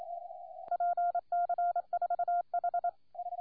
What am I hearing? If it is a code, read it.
Morse code, heard as a keyed tone, about 27 wpm, 1400 Hz, with noise. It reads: PC4H
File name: 4O3A de PC4H CQWWCW CW 14MHz